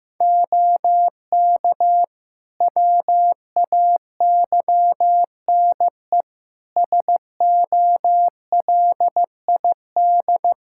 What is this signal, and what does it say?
Morse code, heard as a keyed tone, about 15 wpm, 700 Hz. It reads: OK WAYNE SOLID
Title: Practice QSO - Exchange 3 - More Info - 15wpm